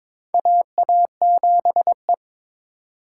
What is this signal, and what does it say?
Morse code, heard as a keyed tone, about 22 wpm, 700 Hz. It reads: AA7E
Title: USA Call Signs - 4 Characters - 2x Character Spacing - 22wpm